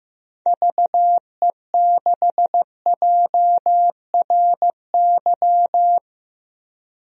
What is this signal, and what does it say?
Morse code, heard as a keyed tone, about 15 wpm, 700 Hz. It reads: VE6JRY
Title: DXCC-100 Call Signs - 6 Characters - Repeated 3x - 15wpm